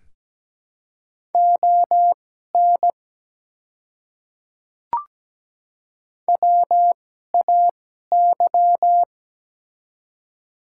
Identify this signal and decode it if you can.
Morse code, heard as a keyed tone, about 17 wpm, spaced as for 12 wpm, 700 Hz. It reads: ON WAY
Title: 2x Character Spacing - Top 100 Words - 17wpm